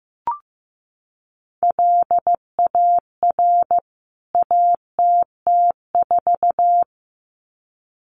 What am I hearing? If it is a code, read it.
Morse code, heard as a keyed tone, about 15 wpm, 700 Hz. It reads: LAR ATT4
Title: CWT Exchange 15wpm